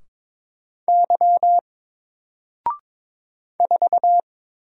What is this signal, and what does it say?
Morse code, heard as a keyed tone, about 22 wpm, 700 Hz. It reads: Y 4
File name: Single-Letter-Number-22wpm